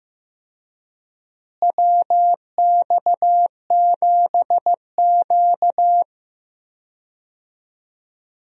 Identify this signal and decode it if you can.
Morse code, heard as a keyed tone, about 15 wpm, 700 Hz. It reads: WX7Q
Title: USA Call Signs - 4 Characters - Repeated 3x - 15wpm